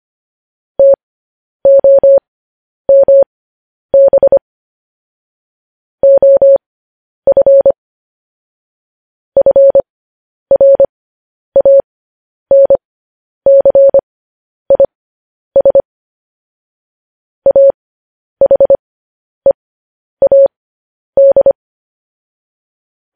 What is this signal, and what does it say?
Morse code, heard as a keyed tone, about 25 wpm, spaced as for 10 wpm, 550 Hz. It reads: TOMB OF FRANCIS AHEAD
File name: podcast / News headlines-1x-25x10